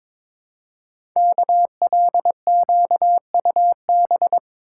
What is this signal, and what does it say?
Morse code, heard as a keyed tone, about 22 wpm, 700 Hz. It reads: KLQUB
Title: Code Groups - 5 Letter:Numbers - 22wpm